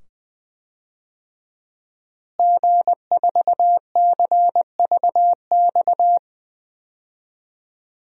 Morse code, heard as a keyed tone, about 21 wpm, 700 Hz. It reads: G4CVX
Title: DXCC-100 Call Signs - 5 Characters - Repeated 2x - 20wpm